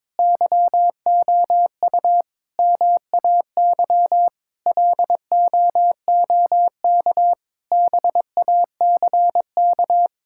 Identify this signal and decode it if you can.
Morse code, heard as a keyed tone, about 22 wpm, 700 Hz. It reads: YOU MAY LOOK BACK
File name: Difficult Sentences with Hints - 22wpm